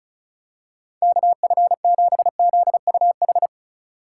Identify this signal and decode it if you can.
Morse code, heard as a keyed tone, about 31 wpm, 700 Hz. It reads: KF7ZUH
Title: Speed Racing - USA 2x3 Call Signs 30wpm